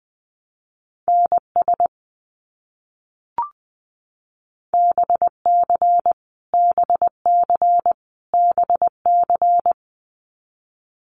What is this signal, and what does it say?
Morse code, heard as a keyed tone, about 20 wpm, 700 Hz. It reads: NS BC BC BC
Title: Canadian Provinces and Territories - Repeated 3x - 20wpm